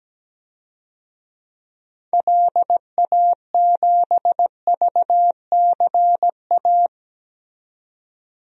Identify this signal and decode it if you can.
Morse code, heard as a keyed tone, about 17 wpm, 700 Hz. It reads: LA7VCA
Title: DXCC-100 Call Signs - 6 Characters - 17wpm